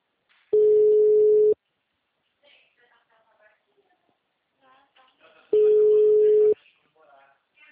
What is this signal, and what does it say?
Ringback tone (Europe, 425 Hz), with speech.